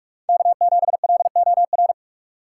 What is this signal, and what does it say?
Morse code, heard as a keyed tone, about 46 wpm, 700 Hz. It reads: K7LOR
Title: USA Call Signs - 5 Characters - 4x Character Spacing - 45wpm